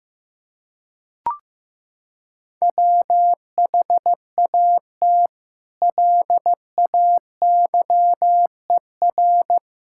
Morse code, heard as a keyed tone, about 15 wpm, 700 Hz. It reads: WHAT LAYER
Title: Sentences from Top 1600 Words 15wpm